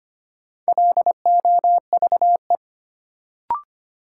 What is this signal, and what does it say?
Morse code, heard as a keyed tone, about 25 wpm, 700 Hz. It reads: LOVE
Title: Sets of 5 Words Encore - New Word Drill - 25wpm